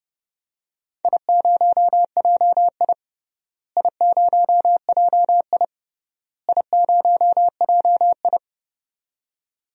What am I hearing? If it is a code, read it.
Morse code, heard as a keyed tone, about 31 wpm, 700 Hz. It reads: I0JI I0JI I0JI
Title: DXCC-100 Call Signs - 4 Characters - Repeated 3x - 30wpm